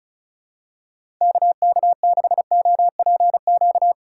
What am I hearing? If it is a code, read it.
Morse code, heard as a keyed tone, about 34 wpm, 700 Hz. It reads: KK6OPQ
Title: USA Call Signs - 2x Character Spacing - 35wpm